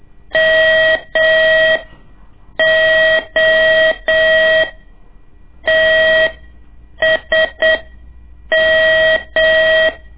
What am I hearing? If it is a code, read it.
Morse code, heard as a keyed tone, about 8 wpm, spaced as for 6 wpm, 640 Hz. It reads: MOTSM
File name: ik1weg2 / CORSI CW / lez2-1